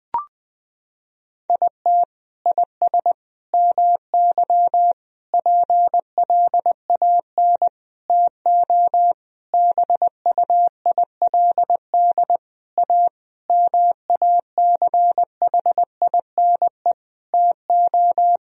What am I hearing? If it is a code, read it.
Morse code, heard as a keyed tone, about 20 wpm, 700 Hz. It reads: IT IS MY PLAN TO BUILD A MACHINE TO
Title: Sentences from Top 1000 Words - Review Part 2 - 20wpm